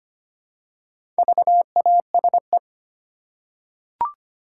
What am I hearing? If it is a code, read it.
Morse code, heard as a keyed tone, about 25 wpm, 690 Hz. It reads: VASE
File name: practice / 25wpm / Easy Words with Hints - 25wpm